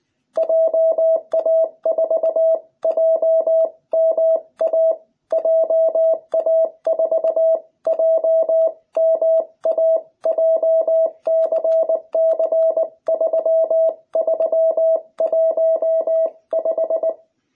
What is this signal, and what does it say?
Morse code, heard as a keyed tone, about 20 wpm, 660 Hz. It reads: JA4JMA JA4JMA JCC3315